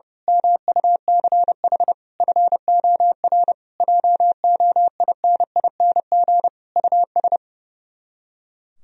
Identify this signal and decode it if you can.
Morse code, heard as a keyed tone, about 30 wpm, 700 Hz. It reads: MUCH FOR JOINING US
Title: Speed-Racing - Sets of 5 Words 30wpm